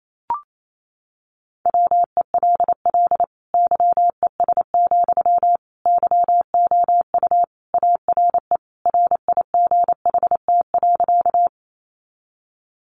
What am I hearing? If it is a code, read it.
Morse code, heard as a keyed tone, about 28 wpm, 700 Hz. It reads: WELL YES, YOU ARE RIGHT.
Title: Sentences from Top 500 Words 28wpm